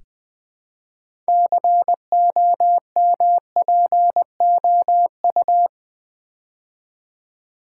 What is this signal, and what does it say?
Morse code, heard as a keyed tone, about 20 wpm, 700 Hz. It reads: COMPOU
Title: Build A Word - 3 Letters At A Time - 8-Letter Words - 20wpm